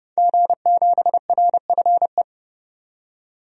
Morse code, heard as a keyed tone, about 30 wpm, 700 Hz. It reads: G7RFE
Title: DXCC-100 Call Signs - 5 Characters - 30wpm